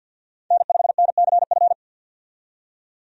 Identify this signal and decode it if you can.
Morse code, heard as a keyed tone, about 48 wpm, 700 Hz. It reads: N5NCF
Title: USA Call Signs - 5 Characters - 50wpm